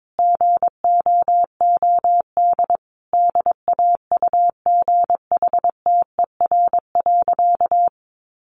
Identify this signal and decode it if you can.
Morse code, heard as a keyed tone, about 22 wpm, 700 Hz. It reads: GOOD DAUGHTER.
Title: Sentences from Top 1500 Words 22wpm